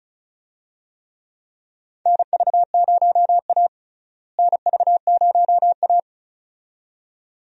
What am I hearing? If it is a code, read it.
Morse code, heard as a keyed tone, about 36 wpm, 700 Hz. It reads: NV0A NV0A
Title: USA Call Signs - 4 Characters - Repeated 2x - 35wpm